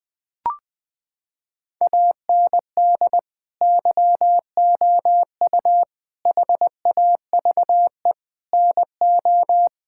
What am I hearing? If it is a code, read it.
Morse code, heard as a keyed tone, about 21 wpm, 700 Hz. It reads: AND YOU HAVE NO
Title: Sentences from Top 1000 Words - Review Part 6 - 20wpm